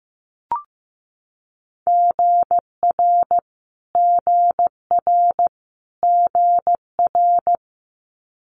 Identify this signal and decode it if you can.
Morse code, heard as a keyed tone, about 15 wpm, 700 Hz. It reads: GR GR GR
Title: Multi-letter Phonemes Repeated 3x 15wpm